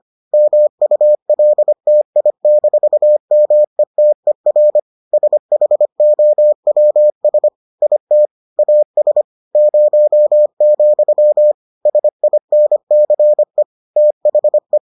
Morse code, heard as a keyed tone, about 25 wpm, 600 Hz. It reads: MULTI-METER SHOWS IT AS 0, SINCE THE